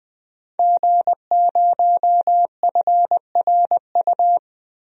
Morse code, heard as a keyed tone, about 21 wpm, 700 Hz. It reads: G0FRU
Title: DXCC-100 Call Signs - 5 Characters - 20wpm